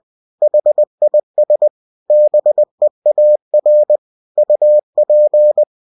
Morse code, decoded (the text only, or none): HIS BEAR UP